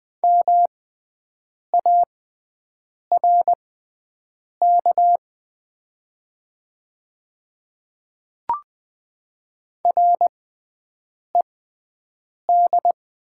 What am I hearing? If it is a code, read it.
Morse code, heard as a keyed tone, about 20 wpm, spaced as for 7 wpm, 700 Hz. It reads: MARK RED
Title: 6x Character Spacing - Top 500 Words - 20wpm